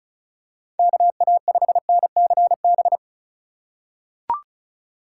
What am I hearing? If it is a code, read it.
Morse code, heard as a keyed tone, about 34 wpm, 700 Hz. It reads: KA5NCB
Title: USA Call Signs - 6 Characters - 3x Character Spacing - 35wpm